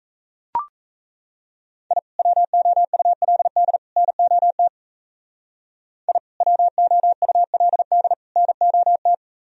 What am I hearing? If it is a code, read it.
Morse code, heard as a keyed tone, about 36 wpm, 700 Hz. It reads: I WOULD NOT I WOULD NOT